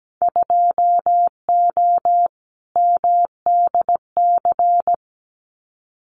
Morse code, heard as a keyed tone, about 17 wpm, 700 Hz. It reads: 2O MDC
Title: Winter Field Day Exchanges 17wpm